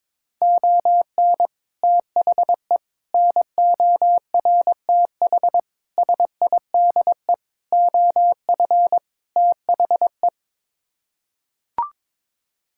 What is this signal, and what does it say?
Morse code, heard as a keyed tone, about 22 wpm, 700 Hz. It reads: ON THE NORTH SIDE OF THE
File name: practice / 22wpm / Sets of 6 Words - 22wpm